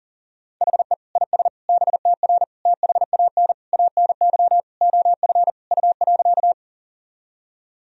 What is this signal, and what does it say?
Morse code, heard as a keyed tone, about 40 wpm, 700 Hz. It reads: HE IS BTR THAN ANY OF U.
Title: Sentences from Top 400 Words and Abbreviations 40wpm